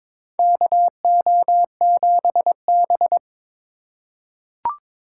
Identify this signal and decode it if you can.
Morse code, heard as a keyed tone, about 22 wpm, 700 Hz. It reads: KO7B
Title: USA Call Signs - 4 Characters - 2x Character Spacing - 22wpm